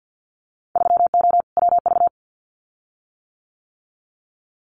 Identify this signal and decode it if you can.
Morse code, heard as a keyed tone, about 50 wpm, 700 Hz. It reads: 3O WV
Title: Winter Field Day Exchanges 50wpm